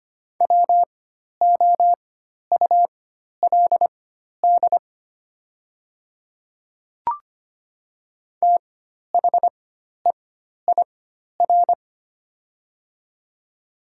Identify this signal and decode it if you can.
Morse code, heard as a keyed tone, about 25 wpm, spaced as for 12 wpm, 700 Hz. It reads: WOULD THEIR
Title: 4x Character Spacing - Top 100 Words - 25wpm